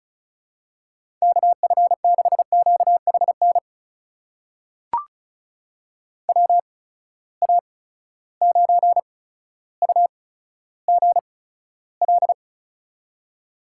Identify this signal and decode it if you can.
Morse code, heard as a keyed tone, about 34 wpm, spaced as for 15 wpm, 700 Hz. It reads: KF6QHN WA9UGL
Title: USA Call Signs - 8x Character Spacing - 35wpm